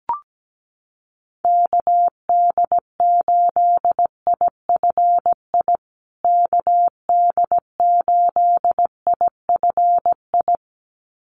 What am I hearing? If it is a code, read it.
Morse code, heard as a keyed tone, about 17 wpm, 700 Hz. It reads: KD8IFI KD8IFI
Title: USA Call Signs Repeated 2x 17wpm